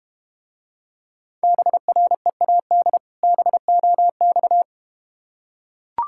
Morse code, heard as a keyed tone, about 32 wpm, 700 Hz. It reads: BREAD BOX